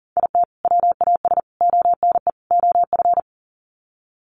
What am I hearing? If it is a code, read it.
Morse code, heard as a keyed tone, about 40 wpm, 700 Hz. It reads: IT WAS ONE OF
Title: Sets of 4 Words 40wpm